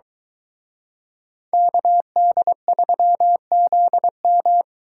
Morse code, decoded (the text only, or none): KD3ZM